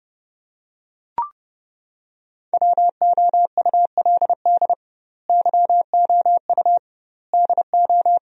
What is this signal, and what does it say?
Morse code, heard as a keyed tone, about 30 wpm, 700 Hz. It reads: WOULD YOU DO
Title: 2x Word-Spacing - Sentences from Top 100 Words - 30wpm